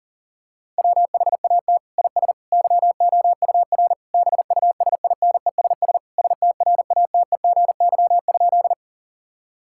Morse code, heard as a keyed tone, about 37 wpm, 700 Hz. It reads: WHAT IS YOUR BUSINESS STRATEGY?
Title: Sentences from Top 1000 Words - Review Part 5 - 40wpm